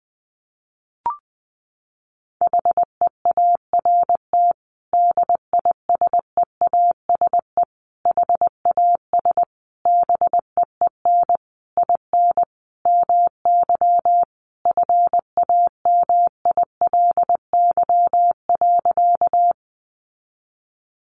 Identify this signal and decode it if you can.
Morse code, heard as a keyed tone, about 19 wpm, 700 Hz. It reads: HEART DISEASE HAS BEEN IN MY FAMILY.
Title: Sentences from Top 1200 Words 20wpm